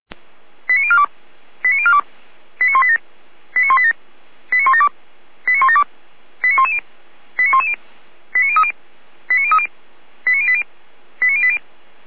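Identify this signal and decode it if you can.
Five-tone selective call (ZVEI), heard as quick runs of five tones, tones of about 70 ms. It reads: 89042 89042 89187 89187 89172 89172 89110 89110 89030 89030 89080 89080
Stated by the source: ZVEI five-tone selective call 89025 received by SDR.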